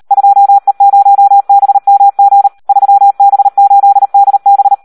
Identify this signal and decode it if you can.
Morse code, heard as a keyed tone, about 38 wpm, 800 Hz. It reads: 2E0BMG 3B8DB
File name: Audio practica CW / SD 1 38